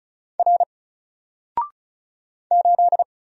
Morse code, heard as a keyed tone, about 35 wpm, 700 Hz. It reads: R 8
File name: Single-Letter-Number-35wpm